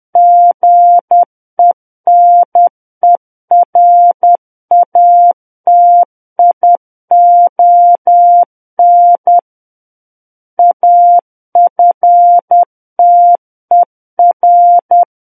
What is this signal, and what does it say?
Morse code, heard as a keyed tone, about 10 wpm, 700 Hz. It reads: GENERATION AFTER
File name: Book One / 01-Chapter 1 - 10 WPM